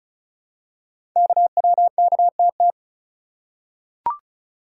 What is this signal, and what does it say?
Morse code, heard as a keyed tone, about 36 wpm, 700 Hz. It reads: KWKTT
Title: Code Groups - 5 Letters - 35wpm